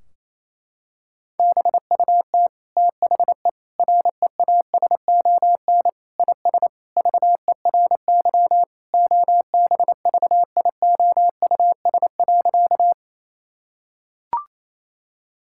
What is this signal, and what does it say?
Morse code, heard as a keyed tone, about 28 wpm, 700 Hz. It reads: BUT THE REASON IS VERY OBVIOUS.
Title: Sentences from Top 1600 Words 28wpm